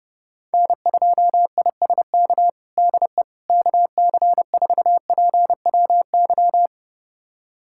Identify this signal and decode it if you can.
Morse code, heard as a keyed tone, about 32 wpm, 700 Hz. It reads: N2ISK DE KC4PWY
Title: USA Callsign DE Callsign 30wpm